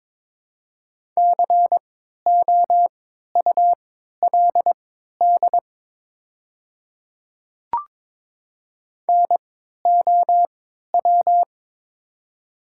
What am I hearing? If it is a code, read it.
Morse code, heard as a keyed tone, about 22 wpm, spaced as for 13 wpm, 690 Hz. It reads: COULD NOW
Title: 3x Character Spacing - Top 100 Words - 22wpm